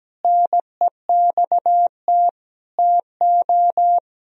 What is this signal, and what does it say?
Morse code, heard as a keyed tone, about 17 wpm, 700 Hz. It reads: NEXT TO
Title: Sentences from Top 1000 Words - Review Part 2 - 17wpm